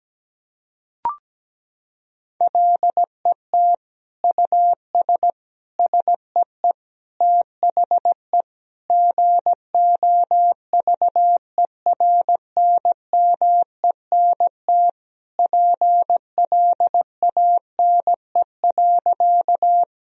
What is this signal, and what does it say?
Morse code, heard as a keyed tone, about 17 wpm, 700 Hz. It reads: LET US SEE THE GOVERNMENT PLANE.
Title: Sentences from Top 1000 Words - Review Part 3 - 17wpm